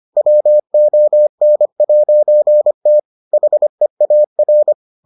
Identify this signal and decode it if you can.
Morse code, heard as a keyed tone, about 25 wpm, 600 Hz. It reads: WON'T HEAR